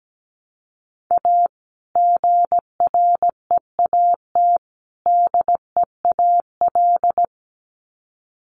Morse code, heard as a keyed tone, about 17 wpm, 700 Hz. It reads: A GREAT DEAL
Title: Sets of 3 Words 17wpm